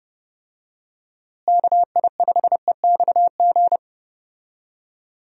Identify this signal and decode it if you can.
Morse code, heard as a keyed tone, about 30 wpm, 700 Hz. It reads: KI5EXG
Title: USA Call Signs - 6 Characters - Repeated 2x - 30wpm